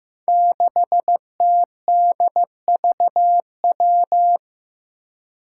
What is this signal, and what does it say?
Morse code, heard as a keyed tone, about 15 wpm, 700 Hz. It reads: 6TDVW
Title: Five Challenging Letter-Number Combinations 15wpm